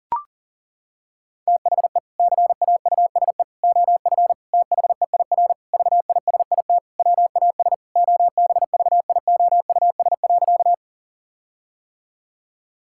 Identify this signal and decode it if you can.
Morse code, heard as a keyed tone, about 37 wpm, 700 Hz. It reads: THE CAUSE OF THEIR VISIT WAS OBVIOUS.
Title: Sentences from Top 1600 Words 40wpm